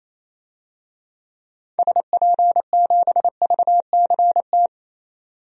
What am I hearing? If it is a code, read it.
Morse code, heard as a keyed tone, about 28 wpm, 700 Hz. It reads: SP7VCT
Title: DXCC-100 Call Signs - 6 Characters - 28wpm